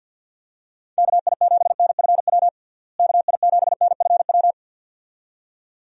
Morse code, heard as a keyed tone, about 51 wpm, 700 Hz. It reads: KI7NFW KI7NFW
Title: USA Call Signs - 6 Characters - Repeated 2x - 50wpm